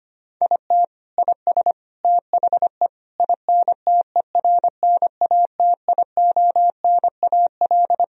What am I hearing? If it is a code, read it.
Morse code, heard as a keyed tone, about 25 wpm, 700 Hz. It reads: IT IS THE INTERNATIONAL
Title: Sentences from Top 1000 Words - Review Part 4 - 25wpm